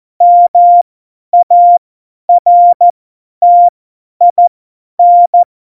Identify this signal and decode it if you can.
Morse code, heard as a keyed tone, about 14 wpm, spaced as for 10 wpm, 700 Hz. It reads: MARTIN
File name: Morsekurs / Witze / Witz 27. Februar 2018, 50 F70